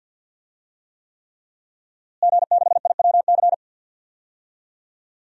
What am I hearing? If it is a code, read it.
Morse code, heard as a keyed tone, about 51 wpm, 700 Hz. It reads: G6IWC
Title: DXCC-100 Call Signs - 5 Characters - Repeated 3x - 50wpm